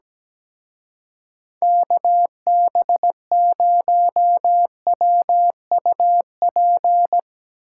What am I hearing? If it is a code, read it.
Morse code, heard as a keyed tone, about 17 wpm, 700 Hz. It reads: KB0WUP
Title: Speed Racing - USA 2x3 Call Signs 17wpm